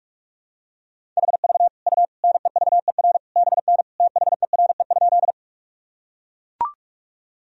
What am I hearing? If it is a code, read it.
Morse code, heard as a keyed tone, about 42 wpm, 700 Hz. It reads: HV U NEVER BN THERE?
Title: Sentences from Top 300 Words and Abbreviations 45wpm